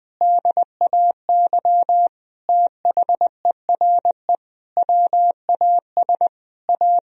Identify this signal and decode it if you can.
Morse code, heard as a keyed tone, about 20 wpm, 700 Hz. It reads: DAY THERE WAS A
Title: Sentences from Top 1000 Words - Review Part 4 - 20wpm